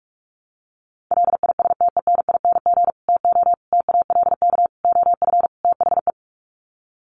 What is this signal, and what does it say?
Morse code, heard as a keyed tone, about 44 wpm, 700 Hz. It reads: LISTENING TO TALK OF THE
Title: Sets of 5 Words 45wpm